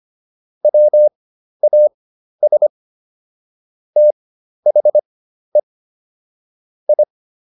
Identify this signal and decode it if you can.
Morse code, heard as a keyed tone, about 25 wpm, spaced as for 12 wpm, 600 Hz. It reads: WAS THE I